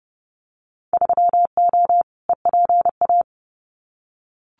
Morse code, heard as a keyed tone, about 30 wpm, 700 Hz. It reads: 3O EPA
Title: Winter Field Day Exchanges Encore 30wpm